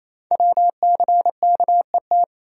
Morse code, heard as a keyed tone, about 27 wpm, 700 Hz. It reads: WCKET
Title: Code Groups - 5 Letters - 28wpm